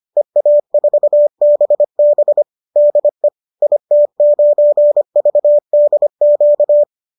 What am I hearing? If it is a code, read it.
Morse code, heard as a keyed tone, about 25 wpm, 590 Hz. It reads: EA4BB DE IT9VDQ